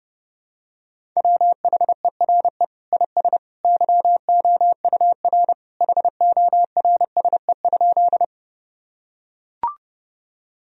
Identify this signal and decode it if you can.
Morse code, heard as a keyed tone, about 30 wpm, 700 Hz. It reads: WHERE IS YOUR HORSE?
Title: Sentences from Top 1000 Words - Review Part 1 - 30wpm